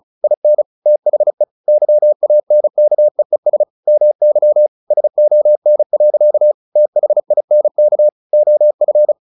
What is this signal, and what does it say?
Morse code, heard as a keyed tone, about 34 wpm, 600 Hz. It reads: IN THE YANKEES MY SON. THINK OF
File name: the old man and the sea / the old man and the sea 1 35wpm